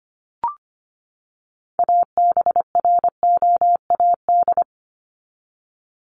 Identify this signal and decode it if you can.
Morse code, heard as a keyed tone, about 25 wpm, 700 Hz. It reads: ABROAD
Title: Top 1751-2000 Words 25wpm